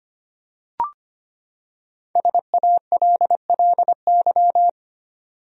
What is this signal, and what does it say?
Morse code, heard as a keyed tone, about 25 wpm, 700 Hz. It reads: SALLY
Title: USA Common Names - Max 5 Letters - 25wpm